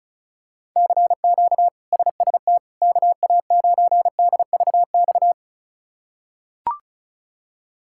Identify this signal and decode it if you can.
Morse code, heard as a keyed tone, about 34 wpm, 700 Hz. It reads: CQ SST KA9DVX
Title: K1USN SST QSO Practice - 35wpm